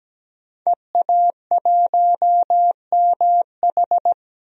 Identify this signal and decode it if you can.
Morse code, heard as a keyed tone, about 17 wpm, 700 Hz. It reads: EA1MH
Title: DXCC-100 Call Signs - 5 Characters - Repeated 3x - 17wpm